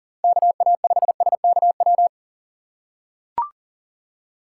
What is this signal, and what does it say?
Morse code, heard as a keyed tone, about 40 wpm, 700 Hz. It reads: KA5SKW
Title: USA Call Signs - 6 Characters - 6x Character Spacing - 40wpm